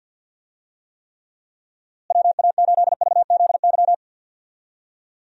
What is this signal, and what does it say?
Morse code, heard as a keyed tone, about 49 wpm, 700 Hz. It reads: WA8VZY
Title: USA Call Signs - 6 Characters - Repeated 3x - 50wpm